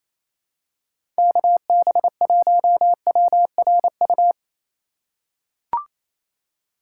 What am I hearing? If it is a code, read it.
Morse code, heard as a keyed tone, about 28 wpm, 690 Hz. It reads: KB1WRU
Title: USA Call Signs - 6 Characters - 4x Character Spacing - 28wpm